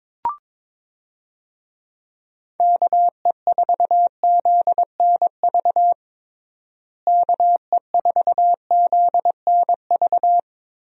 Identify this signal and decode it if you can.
Morse code, heard as a keyed tone, about 22 wpm, 700 Hz. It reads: KE4ZNV KE4ZNV
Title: USA Call Signs - 6 Characters - Repeated 3x - 22wpm